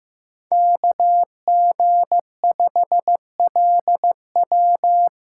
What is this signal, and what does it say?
Morse code, heard as a keyed tone, about 15 wpm, 700 Hz. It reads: KG5LW
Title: USA Call Signs - 5 Characters - 4x Character Spacing - 15wpm